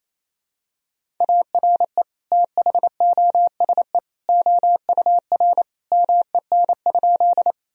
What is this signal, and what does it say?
Morse code, heard as a keyed tone, about 28 wpm, 700 Hz. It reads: ARE THOSE OUR MEN?
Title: Sentences from Top 1000 Words - Review Part 1 - 28wpm